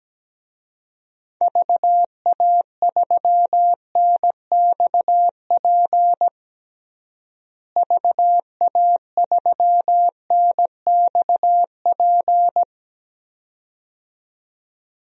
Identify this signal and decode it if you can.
Morse code, heard as a keyed tone, about 17 wpm, 700 Hz. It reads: VA3NXP VA3NXP
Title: DXCC-100 Call Signs - 6 Characters - Repeated 2x - 17wpm